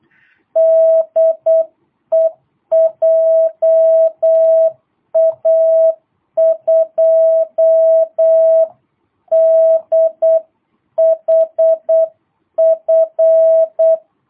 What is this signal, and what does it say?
Morse code, heard as a keyed tone, about 8 wpm, 650 Hz. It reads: DE JA2DHF
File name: jh8cbh / 2022cw